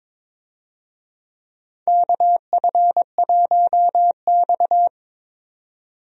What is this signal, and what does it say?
Morse code, heard as a keyed tone, about 22 wpm, 700 Hz. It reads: KF1X
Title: USA Call Signs - 4 Characters - 22wpm